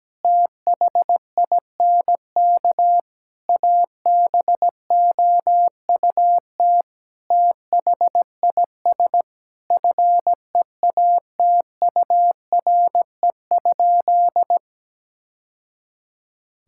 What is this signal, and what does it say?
Morse code, heard as a keyed tone, about 17 wpm, 700 Hz. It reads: THINK ABOUT THIS FEATURE?
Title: Sentences from Top 1000 Words - Review Part 5 - 17wpm